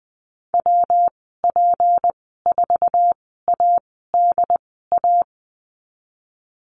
Morse code, heard as a keyed tone, about 20 wpm, spaced as for 15 wpm, 700 Hz. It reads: WP4ADA
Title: 167 Lesson 27g - Number 7 - Call Signs 20wpm